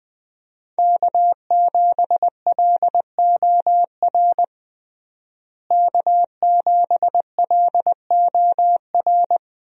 Morse code, heard as a keyed tone, about 20 wpm, 700 Hz. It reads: K7LOR K7LOR
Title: USA Call Signs - 5 Characters - Repeated 2x - 20wpm